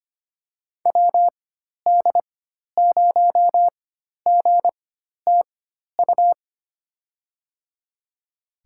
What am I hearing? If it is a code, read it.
Morse code, heard as a keyed tone, about 25 wpm, spaced as for 12 wpm, 700 Hz. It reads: WD0GTU
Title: USA Call Signs - 6 Characters - 4x Character Spacing - 25wpm